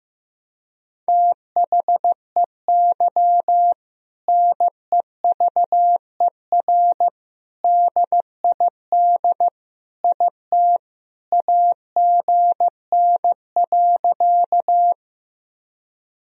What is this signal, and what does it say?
Morse code, heard as a keyed tone, about 14 wpm, 700 Hz. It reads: THEY NEVER DID IT AGN.
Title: Sentences from Top 300 Words and Abbreviations 15wpm